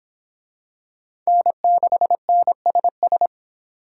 Morse code, heard as a keyed tone, about 26 wpm, 700 Hz. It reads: N6NSS